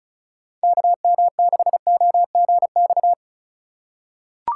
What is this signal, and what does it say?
Morse code, heard as a keyed tone, about 36 wpm, 700 Hz. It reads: KM6OGX
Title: USA Call Signs - 6 Characters - 2x Character Spacing - 35wpm